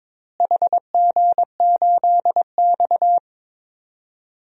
Morse code, heard as a keyed tone, about 21 wpm, 700 Hz. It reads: HG8X
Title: DXCC-100 Call Signs - 4 Characters - Repeated 3x - 22wpm